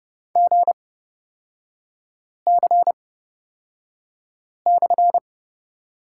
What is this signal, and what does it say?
Morse code, heard as a keyed tone, about 30 wpm, spaced as for 5 wpm, 690 Hz. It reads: GC/
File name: Lesson 37e - Letter x - Three Characters 30wpm